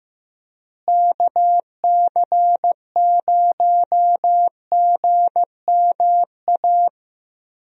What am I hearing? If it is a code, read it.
Morse code, heard as a keyed tone, about 15 wpm, 700 Hz. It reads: KC0GMA
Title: USA Call Signs - 6 Characters - Repeated 2x - 15wpm